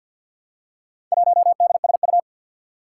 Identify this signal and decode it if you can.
Morse code, heard as a keyed tone, about 53 wpm, 700 Hz. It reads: 1DSU
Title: Four Challenging Letter-Number Combinations 50wpm